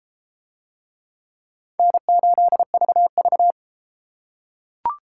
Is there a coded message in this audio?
Yes: Morse code from 1.8 to 3.5 s, heard as a keyed tone, about 30 wpm, 700 Hz. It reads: N8VV